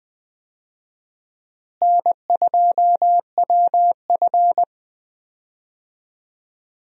Morse code, heard as a keyed tone, about 20 wpm, 700 Hz. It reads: N2WF
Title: USA Call Signs - 4 Characters - Repeated 2x - 20wpm